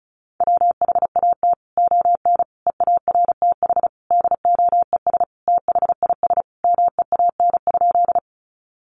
Morse code, heard as a keyed tone, about 34 wpm, 700 Hz. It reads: WHAT ON EARTH DOES THIS MEAN?
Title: Sentences from Top 200 Words 35wpm